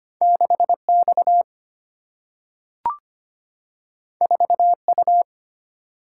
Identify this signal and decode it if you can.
Morse code, heard as a keyed tone, about 25 wpm, 700 Hz. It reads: 6 X 4 U
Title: Single-Letter Single-Number 25wpm